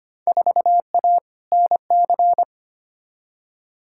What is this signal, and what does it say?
Morse code, heard as a keyed tone, about 25 wpm, 700 Hz. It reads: 4A NC
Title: Field Day Exchanges - 25wpm